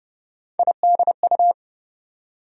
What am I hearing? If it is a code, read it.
Morse code, heard as a keyed tone, about 30 wpm, 700 Hz. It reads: IDU
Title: USA Call Sign Postfixes - 3 Letters 30wpm